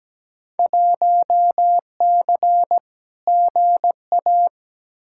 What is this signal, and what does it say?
Morse code, heard as a keyed tone, about 17 wpm, 700 Hz. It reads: 1C GA
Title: Field Day Exchanges Encore - 17wpm